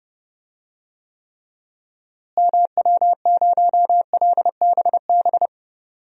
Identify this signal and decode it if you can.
Morse code, heard as a keyed tone, about 32 wpm, 700 Hz. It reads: MW0LBB
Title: DXCC-100 Call Signs - 6 Characters - Repeated 2x - 30wpm